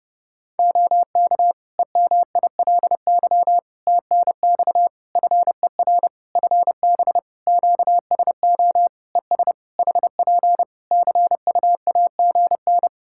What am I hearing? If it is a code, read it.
Morse code, heard as a keyed tone, about 28 wpm, 700 Hz. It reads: OK EMILY TNX FER FB QSO ES HP CUAGN
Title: Practice QSO - Exchange 4 - Ending - 30wpm